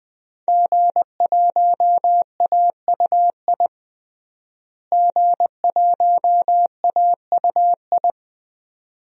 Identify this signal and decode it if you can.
Morse code, heard as a keyed tone, about 20 wpm, 700 Hz. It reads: G1AUI G1AUI
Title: DXCC-100 Call Signs - 5 Characters - Repeated 2x - 20wpm